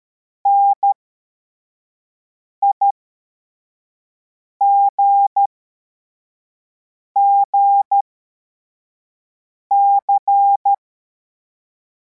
Morse code, heard as a keyed tone, about 13 wpm, spaced as for 4 wpm, 800 Hz. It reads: NIGGC